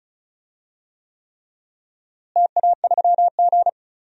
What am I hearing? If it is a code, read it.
Morse code, heard as a keyed tone, about 36 wpm, 700 Hz. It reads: TA3G
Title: DXCC-100 Call Signs - 4 Characters - Repeated 3x - 35wpm